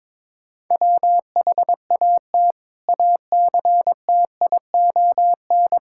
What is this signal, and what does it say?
Morse code, heard as a keyed tone, about 22 wpm, 700 Hz. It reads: WHAT ACTION
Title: Sentences from Top 1000 Words - Review Part 6 - 22wpm